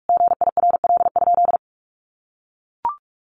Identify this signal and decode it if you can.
Morse code, heard as a keyed tone, about 44 wpm, 700 Hz. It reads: GIRL?
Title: Sentences from Top 800 Words 45wpm